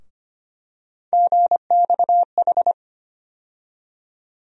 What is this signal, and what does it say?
Morse code, heard as a keyed tone, about 25 wpm, 700 Hz. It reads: GXH
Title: USA Call Sign Postfixes - 3 Letters 25wpm